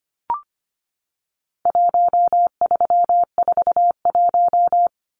Morse code, heard as a keyed tone, about 25 wpm, 700 Hz. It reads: 1341
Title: Numbers - 4-Digit 25wpm